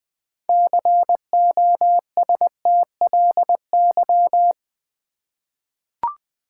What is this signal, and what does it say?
Morse code, heard as a keyed tone, about 20 wpm, 700 Hz. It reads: COSTLY
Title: 6 Letter Words - First 3 Letters Given - 20wpm